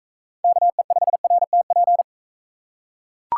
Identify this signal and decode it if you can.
Morse code, heard as a keyed tone, about 40 wpm, 700 Hz. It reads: KE5RTP